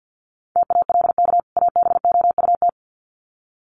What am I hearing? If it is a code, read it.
Morse code, heard as a keyed tone, about 48 wpm, 700 Hz. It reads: TALK ABOUT
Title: Sets of 2 Words 50wpm